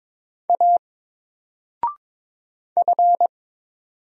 Morse code, heard as a keyed tone, about 22 wpm, 700 Hz. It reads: A F
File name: Single-Letter-Number-22wpm